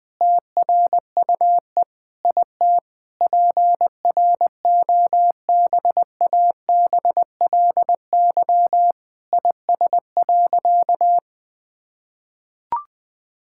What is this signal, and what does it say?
Morse code, heard as a keyed tone, about 20 wpm, 700 Hz. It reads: TRUE IT PROBABLY IS.
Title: Sentences with 9-Word Phrases - 20wpm